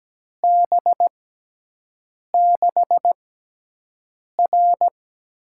Morse code, heard as a keyed tone, about 17 wpm, spaced as for 6 wpm, 700 Hz. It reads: B6R
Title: USA Call Signs - 5 Characters - 6x Character Spacing - 17wpm